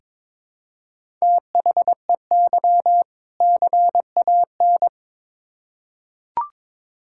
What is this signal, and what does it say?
Morse code, heard as a keyed tone, about 22 wpm, 700 Hz. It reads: THEY CAN
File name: Build A Phrase - 2-Word Phrases - 22wpm